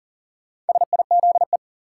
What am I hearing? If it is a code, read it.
Morse code, heard as a keyed tone, about 40 wpm, 700 Hz. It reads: SIZE
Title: Sentences from Top 400 Words - New Word Drill - 40wpm